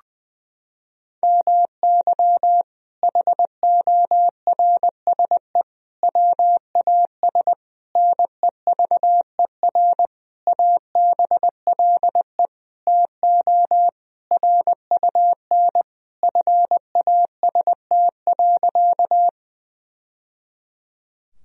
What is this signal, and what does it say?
Morse code, heard as a keyed tone, about 20 wpm, 700 Hz. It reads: MY HORSE WAS NEVER ABLE TO RUN FAST.
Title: Sentences from Top 1000 Words - Review Part 2 - 20wpm